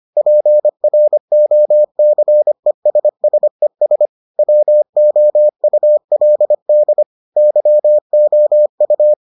Morse code, heard as a keyed tone, about 25 wpm, 600 Hz. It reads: PROCESSES WOULD YOU